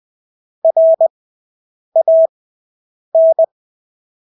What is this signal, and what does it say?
Morse code, heard as a keyed tone, about 20 wpm, spaced as for 8 wpm, 650 Hz. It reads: RAN
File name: Engelske ord 20/8 wpm